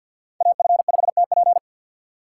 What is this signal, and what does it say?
Morse code, heard as a keyed tone, about 52 wpm, 700 Hz. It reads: AF5TP
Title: USA Call Signs - 5 Characters - 3x Character Spacing - 50wpm